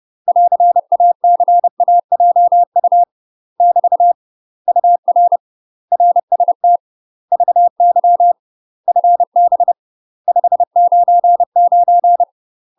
Morse code, heard as a keyed tone, about 30 wpm, 700 Hz. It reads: <AR>ACAJU <BT> UR RST VY FB 599
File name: UK MASTER QSOs TRAINING/30WPM